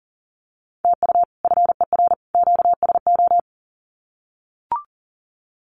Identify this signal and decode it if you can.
Morse code, heard as a keyed tone, about 40 wpm, 700 Hz. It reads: TU FER QSO
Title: Common QSO Phrases 40wpm